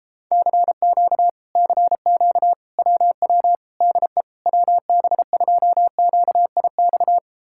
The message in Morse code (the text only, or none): CQ CQ WW DE WB2QIX